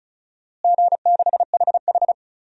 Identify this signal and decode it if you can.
Morse code, heard as a keyed tone, about 36 wpm, 690 Hz. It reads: G6HH
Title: DXCC-100 Call Signs - 4 Characters - Repeated 3x - 35wpm